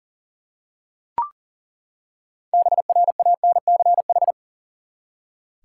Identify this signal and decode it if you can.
Morse code, heard as a keyed tone, about 40 wpm, 700 Hz. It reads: BRANCH
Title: Six Letter Words - 40wpm